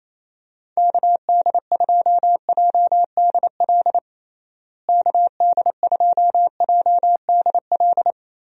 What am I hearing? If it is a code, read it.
Morse code, heard as a keyed tone, about 28 wpm, 700 Hz. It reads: KD2JDL KD2JDL
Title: USA Call Signs - 6 Characters - Repeated 3x - 28wpm